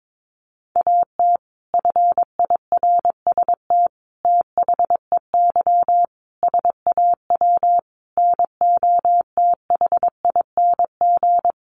Morse code, heard as a keyed tone, about 22 wpm, 700 Hz. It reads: AT FIRST THEY SAW NOTHING
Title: Sentences from Top 600 Words 22wpm